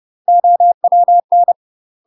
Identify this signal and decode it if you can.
Morse code, heard as a keyed tone, about 30 wpm, 700 Hz. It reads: OWN
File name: UK MASTER QSOs TRAINING / 30WPM / UK MASTER QSOs 0000